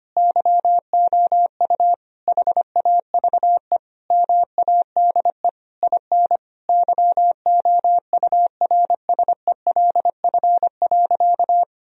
Morse code, heard as a keyed tone, about 25 wpm, 700 Hz. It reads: YOU HAVE MADE IN YOURSELF.
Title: Sentences from Top 1000 Words - Review Part 1 - 25wpm